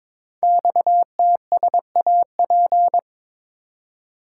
Morse code, heard as a keyed tone, about 23 wpm, 700 Hz. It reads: XTSAP
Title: Code Groups - 5 Letters - 22wpm